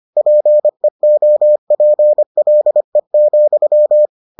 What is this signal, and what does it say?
Morse code, heard as a keyed tone, about 25 wpm, 600 Hz. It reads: PEOPLE,